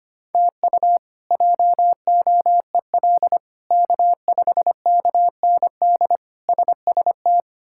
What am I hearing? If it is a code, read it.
Morse code, heard as a keyed tone, about 25 wpm, 700 Hz. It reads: TU JOEL K5KND SST
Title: K1USN SST QSO Practice - 25wpm